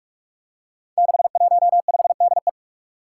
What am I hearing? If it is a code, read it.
Morse code, heard as a keyed tone, about 42 wpm, 700 Hz. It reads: 615DE
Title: Five Challenging Letter-Number Combinations 45wpm